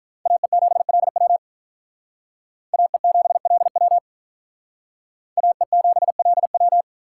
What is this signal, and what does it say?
Morse code, heard as a keyed tone, about 44 wpm, 700 Hz. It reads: AE7LW AE7LW AE7LW